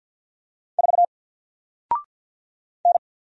Morse code, heard as a keyed tone, about 48 wpm, 700 Hz. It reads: 4 N
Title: Single Character (Letter:Number:Punctuation:Prosign) Alt 50wpm